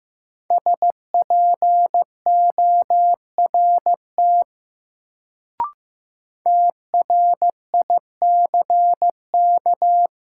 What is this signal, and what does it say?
Morse code, heard as a keyed tone, about 15 wpm, 700 Hz. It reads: SPORT TRICK
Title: Sentences from Top 1500 Words - New Word Drill Alt - 15wpm